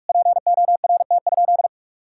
Morse code, heard as a keyed tone, about 46 wpm, 700 Hz. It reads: PORT?
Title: Sentences from Top 1000 Words - Review Part 5 - 45wpm